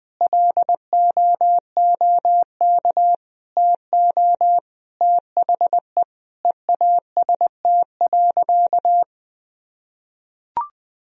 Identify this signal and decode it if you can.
Morse code, heard as a keyed tone, about 21 wpm, 700 Hz. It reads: LOOK TO THE EAST.
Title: Sentences from Top 1000 Words - Review Part 3 - 20wpm